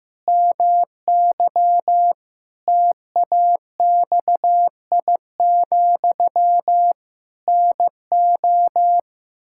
Morse code, heard as a keyed tone, about 15 wpm, 700 Hz. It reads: MY TAXI, NO
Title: Pangram Challenge v2 - 15wpm